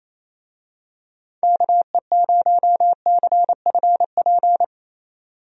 Morse code, heard as a keyed tone, about 28 wpm, 700 Hz. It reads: KE0CFP
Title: USA Call Signs - 6 Characters - 28wpm